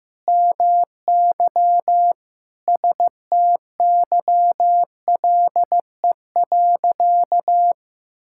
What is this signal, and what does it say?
Morse code, heard as a keyed tone, about 15 wpm, 700 Hz. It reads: MY STYLE.
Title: Sentences from Top 1000 Words - Review Part 5 - 15wpm